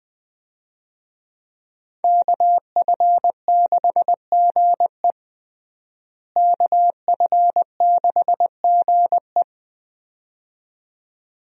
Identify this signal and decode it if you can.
Morse code, heard as a keyed tone, about 20 wpm, 700 Hz. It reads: KF6GE KF6GE
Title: USA Call Signs - 5 Characters - Repeated 2x - 20wpm